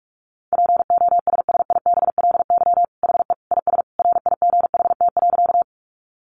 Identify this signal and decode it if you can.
Morse code, heard as a keyed tone, about 44 wpm, 700 Hz. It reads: POSSIBLY HE IS RIGHT.
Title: Sentences from Top 1300 Words 45wpm